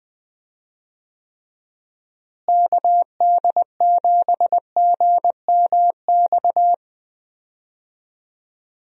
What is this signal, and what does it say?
Morse code, heard as a keyed tone, about 20 wpm, 700 Hz. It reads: KD7GMX
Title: USA Call Signs - 6 Characters - Repeated 3x - 20wpm